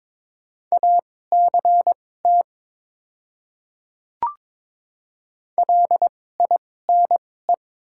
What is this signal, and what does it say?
Morse code, heard as a keyed tone, about 22 wpm, spaced as for 16 wpm, 700 Hz. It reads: ACT LINE
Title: 2x Character Spacing - Top 300 Words - 22wpm